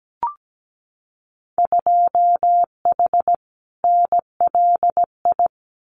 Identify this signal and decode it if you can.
Morse code, heard as a keyed tone, about 17 wpm, 700 Hz. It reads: 2H NLI
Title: Winter Field Day Exchanges 17wpm